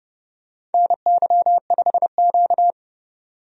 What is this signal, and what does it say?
Morse code, heard as a keyed tone, about 30 wpm, 700 Hz. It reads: NY5Q
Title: USA Call Signs - 4 Characters - 30wpm